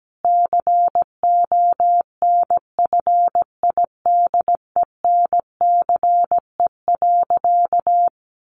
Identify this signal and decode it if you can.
Morse code, heard as a keyed tone, about 17 wpm, 700 Hz. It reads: CONFIDENCE.
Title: Sentences from Top 1500 Words 17wpm